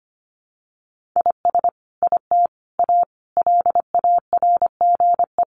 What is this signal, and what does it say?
Morse code, heard as a keyed tone, about 25 wpm, 700 Hz. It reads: IS IT A LARGE
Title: Sentences from Top 1500 Words 25wpm